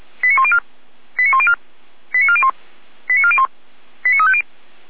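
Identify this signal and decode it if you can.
Five-tone selective call (ZVEI), heard as quick runs of five tones, tones of about 70 ms. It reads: 89104 89104 89401 89401 89350
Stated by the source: ZVEI five-tone call received on SDR, from feuerwehr-orlamuende fire brigade audio server.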